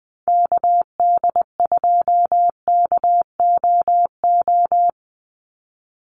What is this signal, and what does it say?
Morse code, heard as a keyed tone, about 20 wpm, 700 Hz. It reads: KD2KOO
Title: USA Call Signs 20wpm